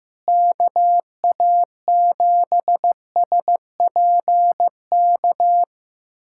Morse code, heard as a keyed tone, about 15 wpm, 700 Hz. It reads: KA7SPK
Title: USA Callsign DE Callsign 15wpm